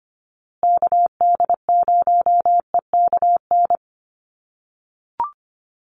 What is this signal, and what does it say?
Morse code, heard as a keyed tone, about 25 wpm, 700 Hz. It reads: KD0EKN
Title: USA Call Signs Repeated 3x 25wpm